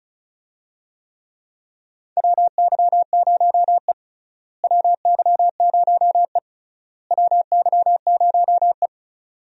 Morse code, heard as a keyed tone, about 34 wpm, 700 Hz. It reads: WY0E WY0E WY0E
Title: USA Call Signs - 4 Characters - Repeated 3x - 35wpm